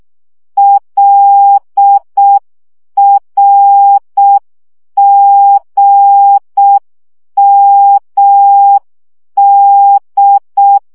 Morse code, heard as a keyed tone, about 6 wpm, 800 Hz. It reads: LRGMD
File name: morze vizsga 2011-1 5